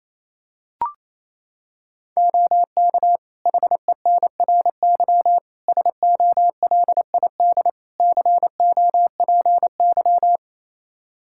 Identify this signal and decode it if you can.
Morse code, heard as a keyed tone, about 28 wpm, 700 Hz. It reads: OK HENRY SOLID COPY
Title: Practice QSO - Exchange 3 - More Info - 28wpm